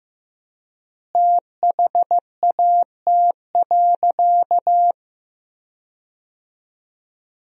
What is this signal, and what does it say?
Morse code, heard as a keyed tone, about 15 wpm, 700 Hz. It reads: THAT.
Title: 4x Word-Spacing - Sentences from Top 300 Words - 15wpm